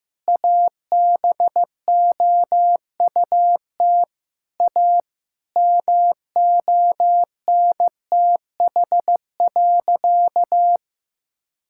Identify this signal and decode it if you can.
Morse code, heard as a keyed tone, about 15 wpm, 700 Hz. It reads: ABOUT A MONTH.
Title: Sentences from Top 1000 Words - Review Part 6 - 15wpm